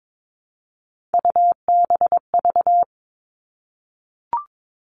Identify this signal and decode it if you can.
Morse code, heard as a keyed tone, about 22 wpm, 700 Hz. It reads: UBV
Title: All Cut Numbers - 3-Digit 22wpm